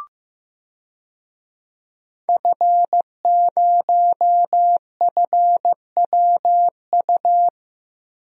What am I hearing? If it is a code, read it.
Morse code, heard as a keyed tone, about 15 wpm, 700 Hz. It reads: F0FWU
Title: DXCC-100 Call Signs - 5 Characters - Repeated 2x - 15wpm